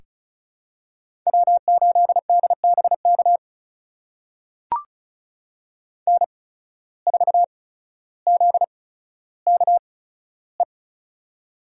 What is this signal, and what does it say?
Morse code, heard as a keyed tone, about 36 wpm, spaced as for 15 wpm, 700 Hz. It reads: W8DBK N4ZKE
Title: USA Call Signs - 5 Characters - 8x Character Spacing - 35wpm